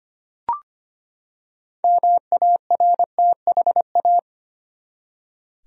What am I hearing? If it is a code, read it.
Morse code, heard as a keyed tone, about 25 wpm, 700 Hz. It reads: MARTHA
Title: USA Common Names - 25wpm